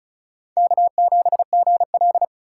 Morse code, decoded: K7GL

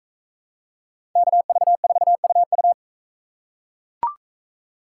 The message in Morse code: KV4UU

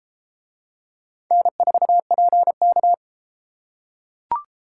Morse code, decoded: N4PK

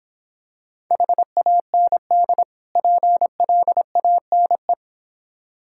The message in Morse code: HAND PLANE

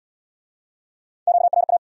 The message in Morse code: QRA

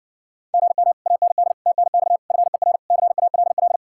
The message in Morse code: GM AND TNX FER CALL

